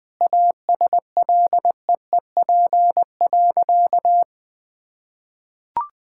Morse code, heard as a keyed tone, about 20 wpm, 700 Hz. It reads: ASLEEP.
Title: Sentences with 3-Word Phrases - 20wpm